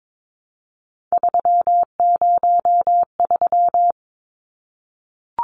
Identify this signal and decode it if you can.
Morse code, heard as a keyed tone, about 22 wpm, 700 Hz. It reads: 303
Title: Numbers - 3-Digit 22wpm